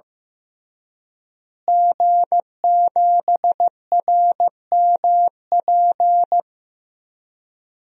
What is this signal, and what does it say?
Morse code, heard as a keyed tone, about 15 wpm, 700 Hz. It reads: G7RMP
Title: DXCC-100 Call Signs - 5 Characters - Repeated 3x - 15wpm